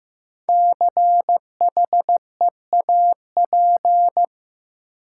Morse code, heard as a keyed tone, about 15 wpm, 700 Hz. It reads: CHEAP
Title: Five Letter Words - 15wpm